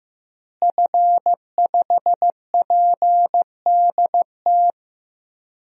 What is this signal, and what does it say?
Morse code, heard as a keyed tone, about 15 wpm, 700 Hz. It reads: F5PDT
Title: DXCC-100 Call Signs - 5 Characters - Repeated 2x - 15wpm